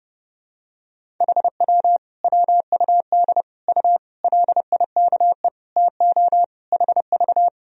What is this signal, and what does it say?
Morse code, heard as a keyed tone, about 30 wpm, 700 Hz. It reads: HW WUD U LIKE TO HV
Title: Sentences from Top 100 Words and Abbreviations 30wpm